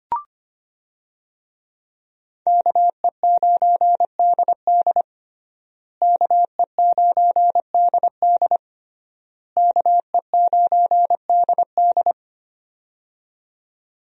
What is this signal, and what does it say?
Morse code, heard as a keyed tone, about 25 wpm, 700 Hz. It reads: KE9DD KE9DD KE9DD
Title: USA Call Signs - 5 Characters - Repeated 3x - 25wpm